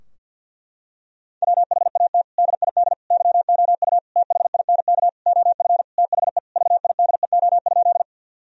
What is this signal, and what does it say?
Morse code, heard as a keyed tone, about 50 wpm, 700 Hz. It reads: WHAT DID YOU THINK OF THE VIDEO?
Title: Sentences from Top 1000 Words - Review Part 4 - 50wpm